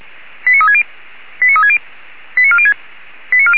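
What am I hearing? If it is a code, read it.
Five-tone selective call (ZVEI), heard as quick runs of five tones, tones of about 70 ms. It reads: 89360 89360 89406
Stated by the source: ZVEI five-tone selective call received by SDR.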